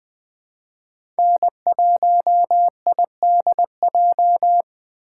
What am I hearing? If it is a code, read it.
Morse code, heard as a keyed tone, about 20 wpm, 700 Hz. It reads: N1IDJ
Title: USA Call Signs - 5 Characters - 3x Character Spacing - 20wpm